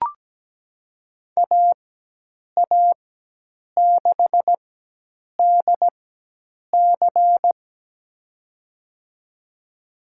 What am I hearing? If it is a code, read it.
Morse code, heard as a keyed tone, about 17 wpm, spaced as for 8 wpm, 700 Hz. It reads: AA6DC
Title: USA Call Signs - 5 Characters - 4x Character Spacing - 17wpm